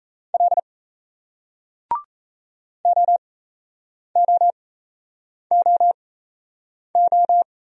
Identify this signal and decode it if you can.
Morse code, heard as a keyed tone, about 38 wpm, spaced as for 8 wpm, 700 Hz. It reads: L OOOO